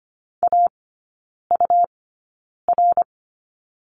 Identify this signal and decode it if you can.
Morse code, heard as a keyed tone, about 25 wpm, spaced as for 9 wpm, 700 Hz. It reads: AUR
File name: Lesson 17e - Letter u - Three Characters 25wpm